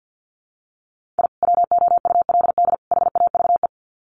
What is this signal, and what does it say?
Morse code, heard as a keyed tone, about 50 wpm, 700 Hz. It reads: I WOULD HAVE
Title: Sets of 3 Words 50wpm